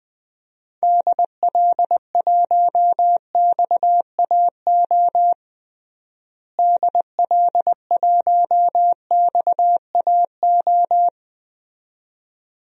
Morse code, heard as a keyed tone, about 20 wpm, 700 Hz. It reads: DL1XAO DL1XAO
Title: DXCC-100 Call Signs - 6 Characters - Repeated 2x - 20wpm